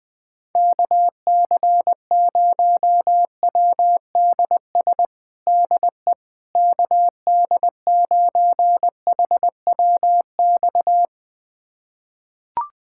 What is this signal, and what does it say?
Morse code, heard as a keyed tone, about 20 wpm, 700 Hz. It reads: KC0WDS DE KD9HWX
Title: USA Callsign DE Callsign - 6 Characters - 20wpm